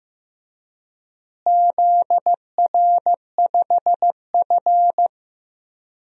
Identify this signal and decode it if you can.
Morse code, heard as a keyed tone, about 15 wpm, 700 Hz. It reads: ZR5F
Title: DXCC-100 Call Signs - 4 Characters - Repeated 3x - 15wpm